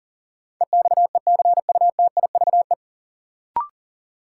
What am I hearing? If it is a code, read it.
Morse code, heard as a keyed tone, about 40 wpm, 700 Hz. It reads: EXECUTIVE
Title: Sets of 5 Words Encore - New Word Drill - 40wpm